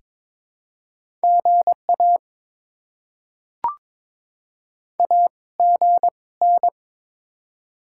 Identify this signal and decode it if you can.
Morse code, heard as a keyed tone, about 22 wpm, spaced as for 18 wpm, 690 Hz. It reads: GA AGN
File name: Top 20 QSO Elements - 2x Character Spacing - 22wpm